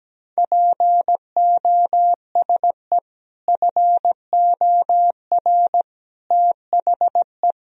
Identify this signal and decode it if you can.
Morse code, heard as a keyed tone, about 17 wpm, 700 Hz. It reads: POSE FOR THE
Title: Sentences from Top 1000 Words - Review Part 2 - 17wpm